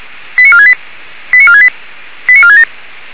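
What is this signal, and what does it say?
Five-tone selective call (ZVEI), heard as quick runs of five tones, tones of about 80 ms. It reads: 89468 89468 89467
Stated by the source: ZVEI five-tone selective call received by SDR, feuerwehr-orlamuende.org (Orlamünde fire brigade).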